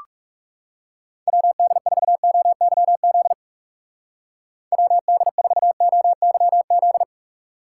Morse code, heard as a keyed tone, about 36 wpm, 700 Hz. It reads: WD4OYZ WD4OYZ
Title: Speed Racing - USA 2x3 Call Signs 30wpm